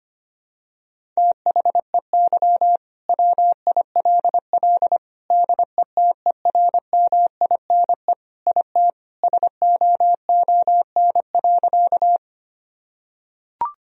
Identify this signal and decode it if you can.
Morse code, heard as a keyed tone, about 25 wpm, 700 Hz. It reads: THEY WILL DETERMINE IT SOON.
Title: Sentences from Top 1000 Words - Review Part 4 - 25wpm